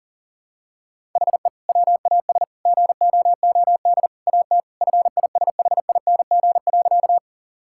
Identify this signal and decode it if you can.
Morse code, heard as a keyed tone, about 40 wpm, 700 Hz. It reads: HE WAS GOOD AT FISHING.
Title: Sentences from Top 1000 Words - Review Part 5 - 40wpm